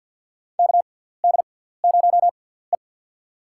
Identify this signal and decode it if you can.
Morse code, heard as a keyed tone, about 47 wpm, spaced as for 17 wpm, 700 Hz. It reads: KD0E
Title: USA Call Signs - 6x Character Spacing - 50wpm